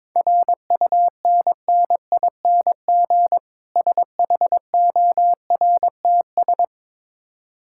Morse code, heard as a keyed tone, about 23 wpm, 690 Hz. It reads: RUNNING SHORTS
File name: Difficult Words with Hints - 22wpm